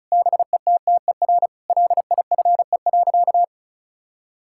Morse code, heard as a keyed tone, about 36 wpm, 700 Hz. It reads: BETTER LIFE.
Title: Sentences from Top 1000 Words - Review Part 5 - 35wpm